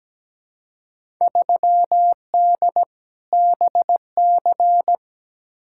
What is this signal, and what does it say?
Morse code, heard as a keyed tone, about 17 wpm, 700 Hz. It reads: 3D BC
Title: Field Day Exchanges Encore - 17wpm